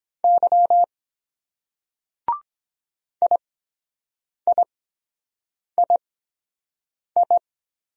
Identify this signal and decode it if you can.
Morse code, heard as a keyed tone, about 22 wpm, spaced as for 7 wpm, 700 Hz. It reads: Y IIII